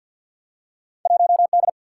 Morse code, heard as a keyed tone, about 53 wpm, 700 Hz. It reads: 1D
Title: Field Day Exchanges - 50wpm